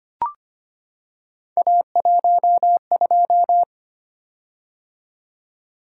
Morse code, heard as a keyed tone, about 25 wpm, 700 Hz. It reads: A12
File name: Three Letter-Number Combinations - 25wpm